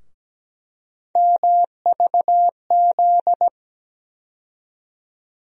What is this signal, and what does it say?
Morse code, heard as a keyed tone, about 17 wpm, 700 Hz. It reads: MVZ
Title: USA Call Sign Postfixes - 3 Letters 17wpm